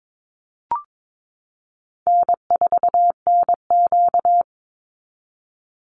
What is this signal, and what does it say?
Morse code, heard as a keyed tone, about 22 wpm, 700 Hz. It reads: N4NQ
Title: Easy Call Signs Repeated 2x - 22wpm